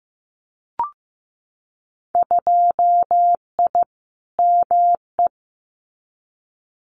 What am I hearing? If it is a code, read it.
Morse code, heard as a keyed tone, about 15 wpm, 700 Hz. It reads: 2I ME
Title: Winter Field Day Exchanges Encore 15wpm